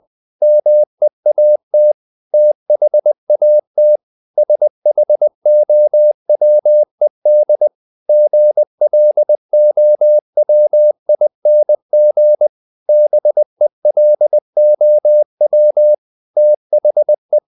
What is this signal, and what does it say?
Morse code, heard as a keyed tone, about 20 wpm, 600 Hz. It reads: MEAT THAT SHOWED GLOWING BELOW THE
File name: the old man and the sea part 2 20wpm